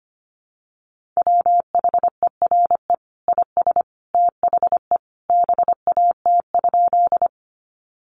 Morse code, heard as a keyed tone, about 25 wpm, 700 Hz. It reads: WHERE IS THE BAT?
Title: Sentences from Top 1500 Words 25wpm